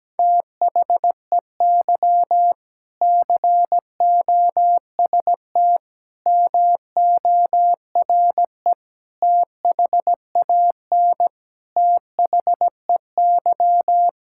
Morse code, heard as a keyed tone, about 17 wpm, 700 Hz. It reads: THEY COST MORE THAN THEY
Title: Sentences from Top 1000 Words - Review Part 5 - 17wpm